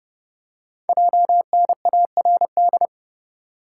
Morse code, heard as a keyed tone, about 30 wpm, 700 Hz. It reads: JNARD
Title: Code Groups - 5 Letters - 30wpm